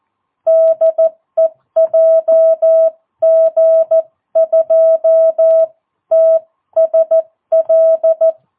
Morse code, heard as a keyed tone, about 14 wpm, spaced as for 12 wpm, 660 Hz. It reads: DE JG2TSL
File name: jh8cbh / 2022cw